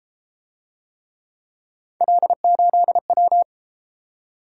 Morse code, heard as a keyed tone, about 30 wpm, 700 Hz. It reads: L8W